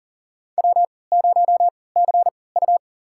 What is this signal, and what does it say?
Morse code, heard as a keyed tone, about 40 wpm, spaced as for 23 wpm, 700 Hz. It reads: W0CU
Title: USA Call Signs - 4 Characters - 3x Character Spacing - 40wpm